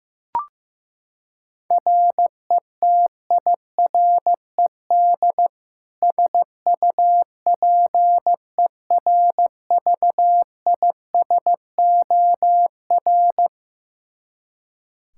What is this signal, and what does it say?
Morse code, heard as a keyed tone, about 15 wpm, 700 Hz. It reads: RETIRED SUPERVISOR
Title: Practice QSO - Exchange 3 - More Info - 15wpm